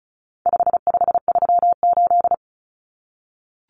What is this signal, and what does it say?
Morse code, heard as a keyed tone, about 34 wpm, 700 Hz. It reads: 5538
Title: Common Cut Numbers - 4-Digit 35wpm